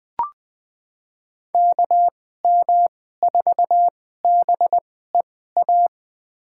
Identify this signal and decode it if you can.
Morse code, heard as a keyed tone, about 20 wpm, spaced as for 14 wpm, 700 Hz. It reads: KM4BEA
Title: USA Call Signs - 2x Character Spacing - 20wpm